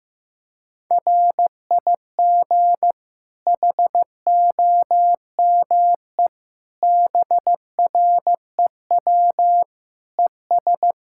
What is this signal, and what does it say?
Morse code, heard as a keyed tone, about 15 wpm, 700 Hz. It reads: RIG HOME BREW ES
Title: Practice QSO - Exchange 2 - Info - 15wpm